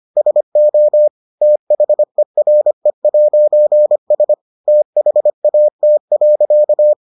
Morse code, heard as a keyed tone, about 25 wpm, 600 Hz. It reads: SO THERE'S THAT.